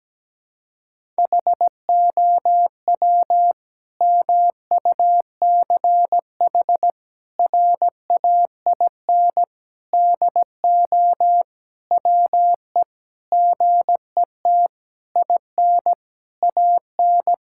Morse code, heard as a keyed tone, about 17 wpm, 700 Hz. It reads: HOW MUCH RAIN DO WE GET IN AN
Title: Sentences from Top 1000 Words - Review Part 6 - 17wpm